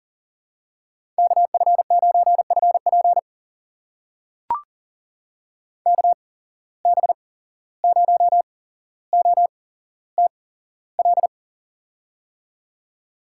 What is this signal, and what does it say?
Morse code, heard as a keyed tone, about 39 wpm, spaced as for 16 wpm, 700 Hz. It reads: KF9FP KB0OTL
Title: USA Call Signs - 8x Character Spacing - 40wpm